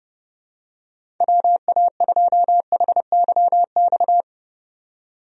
Morse code, heard as a keyed tone, about 30 wpm, 700 Hz. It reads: WA2HYX
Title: USA Call Signs - 6 Characters - 30wpm